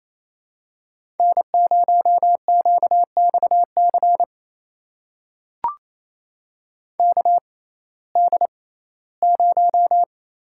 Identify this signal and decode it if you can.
Morse code, heard as a keyed tone, about 28 wpm, spaced as for 17 wpm, 700 Hz. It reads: N0QXC KD0
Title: USA Call Signs - 6x Character Spacing - 28wpm